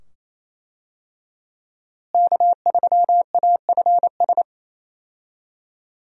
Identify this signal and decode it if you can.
Morse code, heard as a keyed tone, about 28 wpm, 700 Hz. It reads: K3AFS
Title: USA Call Signs - 5 Characters - Repeated 2x - 28wpm